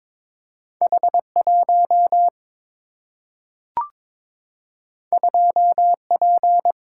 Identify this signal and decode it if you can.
Morse code, heard as a keyed tone, about 22 wpm, 700 Hz. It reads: H1 2P
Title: Single-Letter Single-Number 22wpm